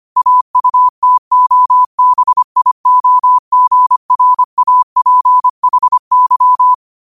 Morse code, heard as a keyed tone, about 25 wpm, 1000 Hz. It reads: AUTOBIOGRAPHY
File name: qotdfiles / 25wpm